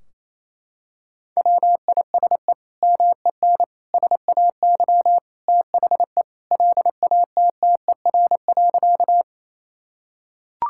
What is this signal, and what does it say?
Morse code, heard as a keyed tone, about 28 wpm, 700 Hz. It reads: WISE MEN SAY THE LATTER.
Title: Sentences from Top 1600 Words 28wpm